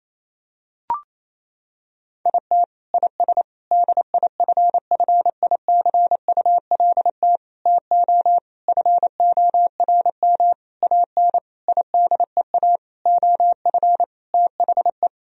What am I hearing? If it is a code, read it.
Morse code, heard as a keyed tone, about 27 wpm, 700 Hz. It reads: IT IS DIFFICULT TO FORM AN IDEA OF THE
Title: Sentences from Top 1000 Words - Review Part 1 - 28wpm